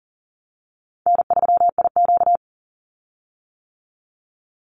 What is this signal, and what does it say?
Morse code, heard as a keyed tone, about 40 wpm, 700 Hz. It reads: N3IQ
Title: Easy Call Signs - 40wpm